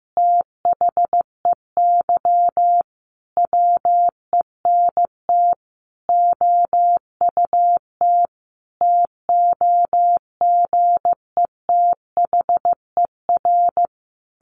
Morse code, heard as a keyed tone, about 15 wpm, 700 Hz. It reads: THEY WENT OUT TOGETHER
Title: Sentences from Top 600 Words 15wpm